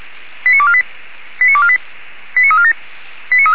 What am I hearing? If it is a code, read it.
Five-tone selective call (ZVEI), heard as quick runs of five tones, tones of about 70 ms. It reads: 89248 89248 89347
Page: SDR/ZVEI